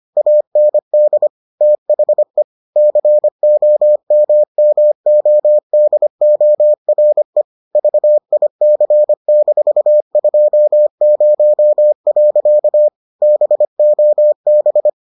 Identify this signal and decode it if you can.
Morse code, heard as a keyed tone, about 25 wpm, 600 Hz. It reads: AND THE COMMODORE VIC-20. BOB